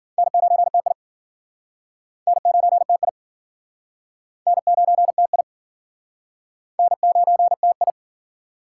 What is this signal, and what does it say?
Morse code, heard as a keyed tone, about 56 wpm, 700 Hz. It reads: N9TI N9TI N9TI N9TI